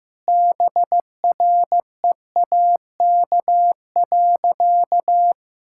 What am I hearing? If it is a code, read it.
Morse code, heard as a keyed tone, about 14 wpm, 700 Hz. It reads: BREAK.
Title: Sentences from Top 1000 Words - Review Part 6 - 15wpm